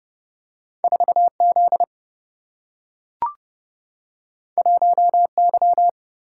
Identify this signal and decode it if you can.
Morse code, heard as a keyed tone, about 30 wpm, 700 Hz. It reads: 4Z 1Y
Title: Single-Letter Single-Number 30wpm